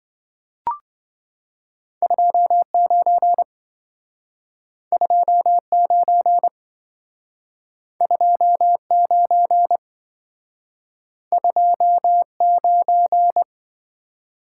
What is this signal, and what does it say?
Morse code, heard as a keyed tone, about 27 wpm, 700 Hz. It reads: 29 29 29 29